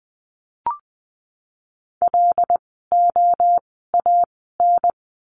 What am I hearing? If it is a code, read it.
Morse code, heard as a keyed tone, about 20 wpm, 700 Hz. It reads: L O A N
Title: 075 Lesson 13f - Letter L - Words 20wpm